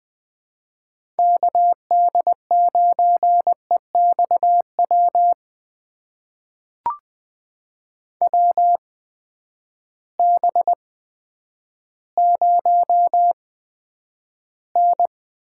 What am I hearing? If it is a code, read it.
Morse code, heard as a keyed tone, about 21 wpm, spaced as for 10 wpm, 700 Hz. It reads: KD9EXW WB0N
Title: USA Call Signs - 6 Characters - 8x Character Spacing - 20wpm